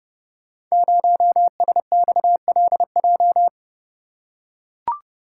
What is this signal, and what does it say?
Morse code, heard as a keyed tone, about 30 wpm, 700 Hz. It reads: 0SXLJ
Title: Code Groups - 5 Letter:Numbers - 30wpm